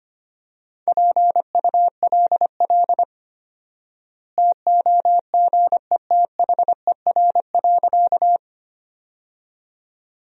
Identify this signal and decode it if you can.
Morse code, heard as a keyed tone, about 25 wpm, 700 Hz. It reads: PULL TOGETHER.
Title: 4x Word-Spacing - Sentences from Top 400 Words - 25wpm